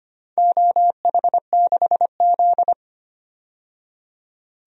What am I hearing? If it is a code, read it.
Morse code, heard as a keyed tone, about 25 wpm, 700 Hz. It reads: OH6Z
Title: DXCC-100 Call Signs - 4 Characters - 25wpm